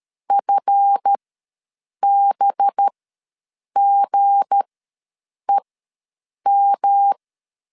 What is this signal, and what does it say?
Morse code, heard as a keyed tone, about 13 wpm, spaced as for 7 wpm, 800 Hz. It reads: FBGEM
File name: LEZIONI / 9 286 ws 12wpm 3351